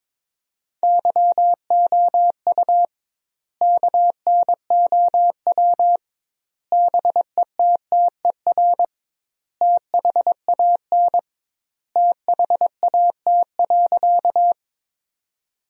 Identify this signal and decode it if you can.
Morse code, heard as a keyed tone, about 22 wpm, 700 Hz. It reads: YOU KNOW BETTER THAN THAT.
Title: 2x Word-Spacing - Sentences from Top 400 Words - 22wpm